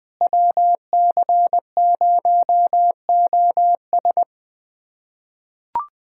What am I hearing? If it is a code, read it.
Morse code, heard as a keyed tone, about 21 wpm, 700 Hz. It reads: WC0OS
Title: USA Callsign DE Callsign - 5 Characters - 20wpm